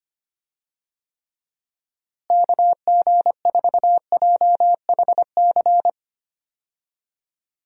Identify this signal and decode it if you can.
Morse code, heard as a keyed tone, about 25 wpm, 700 Hz. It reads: KG4JHC
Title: USA Call Signs - 6 Characters - Repeated 2x - 25wpm